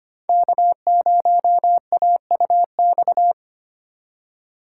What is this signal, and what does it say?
Morse code, heard as a keyed tone, about 31 wpm, 700 Hz. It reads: K0AUX
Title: Speed Racing - USA 1x3 Call Signs 25wpm